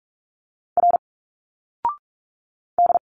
Morse code, heard as a keyed tone, about 46 wpm, 700 Hz. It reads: R D
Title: Single Letters 45wpm